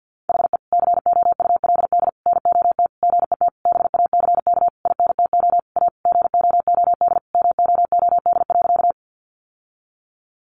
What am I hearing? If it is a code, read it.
Morse code, heard as a keyed tone, about 50 wpm, 700 Hz. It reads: HE COULD NOT GET BACK INTO A GOOD MOOD.
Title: Sentences from Top 1100 Words 50wpm